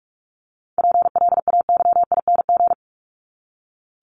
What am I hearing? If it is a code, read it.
Morse code, heard as a keyed tone, about 44 wpm, 700 Hz. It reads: PLAYING
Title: ING Suffix - Top 100 Words with Base Words 45wpm